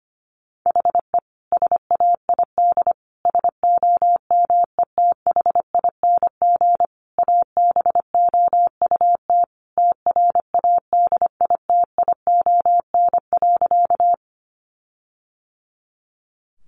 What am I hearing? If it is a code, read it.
Morse code, heard as a keyed tone, about 25 wpm, 700 Hz. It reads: HE SAID SOMETHING ABOUT TRADITION.
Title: Sentences from Top 1500 Words 25wpm